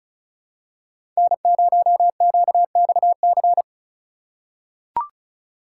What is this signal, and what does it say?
Morse code, heard as a keyed tone, about 36 wpm, 700 Hz. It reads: N0QXC
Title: USA Call Signs - 5 Characters - 3x Character Spacing - 35wpm